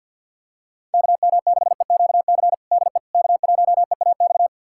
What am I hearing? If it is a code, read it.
Morse code, heard as a keyed tone, about 48 wpm, 700 Hz. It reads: KM6EQC DE K1EAX
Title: USA Callsign DE Callsign 50wpm